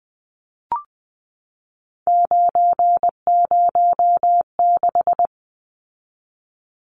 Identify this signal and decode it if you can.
Morse code, heard as a keyed tone, about 20 wpm, 700 Hz. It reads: 906
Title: Numbers - 3-Digit 20wpm